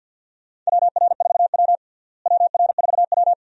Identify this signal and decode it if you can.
Morse code, heard as a keyed tone, about 51 wpm, 700 Hz. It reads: WR4W WR4W
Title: USA Call Signs - 4 Characters - Repeated 3x - 50wpm